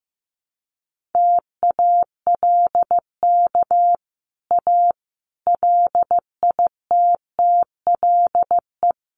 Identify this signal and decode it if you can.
Morse code, heard as a keyed tone, about 15 wpm, 700 Hz. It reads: TALK A LITTLE
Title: Sets of 5 Words Encore 15wpm